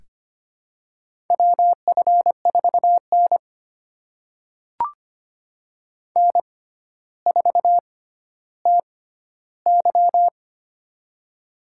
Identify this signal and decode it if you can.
Morse code, heard as a keyed tone, about 25 wpm, spaced as for 13 wpm, 700 Hz. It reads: WF4N N4TY
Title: USA Call Signs - 4 Characters - 6x Character Spacing - 25wpm